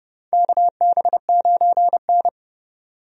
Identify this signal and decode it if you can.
Morse code, heard as a keyed tone, about 30 wpm, 700 Hz. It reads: KB9N
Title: USA Call Signs - 4 Characters - Repeated 2x - 30wpm